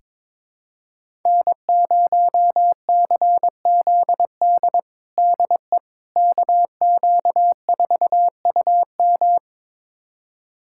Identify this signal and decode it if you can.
Morse code, heard as a keyed tone, about 22 wpm, 700 Hz. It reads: N0CZD DE KQ4UM
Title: USA Callsign DE Callsign - 5 Characters - 22wpm